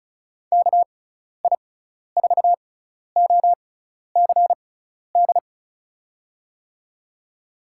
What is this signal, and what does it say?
Morse code, heard as a keyed tone, about 34 wpm, spaced as for 12 wpm, 700 Hz. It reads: KI4OCD
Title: USA Call Signs - 6 Characters - 6x Character Spacing - 35wpm